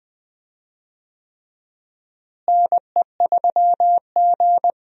Morse code, heard as a keyed tone, about 20 wpm, 700 Hz. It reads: NE3G
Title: USA Call Signs - 4 Characters - Repeated 3x - 20wpm